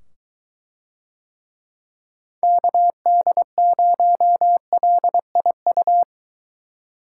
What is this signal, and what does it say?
Morse code, heard as a keyed tone, about 22 wpm, 700 Hz. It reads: KD0LIU